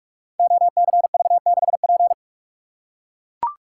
Morse code, heard as a keyed tone, about 44 wpm, 700 Hz. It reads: OCV6P
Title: Code Groups - 5 Letter:Numbers - 45wpm